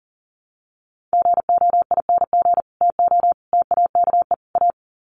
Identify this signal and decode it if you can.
Morse code, heard as a keyed tone, about 40 wpm, 700 Hz. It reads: GOING TO TAKE A
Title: Sets of 4 Words 40wpm